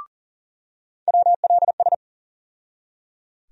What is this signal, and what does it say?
Morse code, heard as a keyed tone, about 38 wpm, 700 Hz. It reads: WLS
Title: USA Call Sign Postfixes - 3 Letters 40wpm